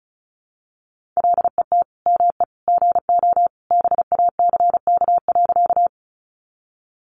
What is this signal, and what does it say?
Morse code, heard as a keyed tone, about 34 wpm, 700 Hz. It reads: LET ME GO BACK.
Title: Sentences from Top 300 Words 35wpm